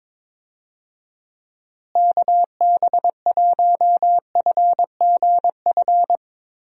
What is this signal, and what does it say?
Morse code, heard as a keyed tone, about 22 wpm, 700 Hz. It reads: KB1FGF
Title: USA Call Signs - 6 Characters - Repeated 2x - 22wpm